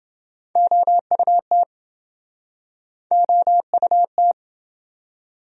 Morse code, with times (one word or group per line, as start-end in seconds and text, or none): 0.6-1.6 OUT
3.1-4.3 OUT